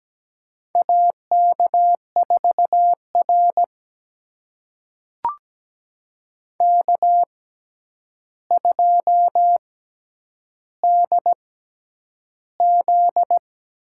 Morse code, heard as a keyed tone, about 17 wpm, spaced as for 9 wpm, 700 Hz. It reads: AK4R K2DZ
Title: USA Call Signs - 4 Characters - 6x Character Spacing - 17wpm